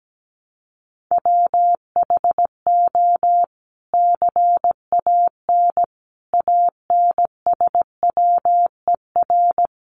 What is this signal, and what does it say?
Morse code, heard as a keyed tone, about 17 wpm, 700 Hz. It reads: WHO CAN ANSWER
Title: Sentences from Top 300 Words 17wpm